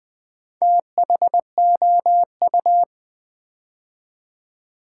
Morse code, heard as a keyed tone, about 20 wpm, 700 Hz. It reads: THOU
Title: Build A Word - 2 Letters At A Time - 6-Letter Words - 20wpm